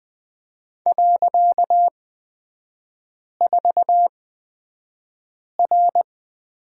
Morse code, heard as a keyed tone, about 20 wpm, spaced as for 5 wpm, 700 Hz. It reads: .4R
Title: 255 Lesson 41e - Period - Three Characters 20wpm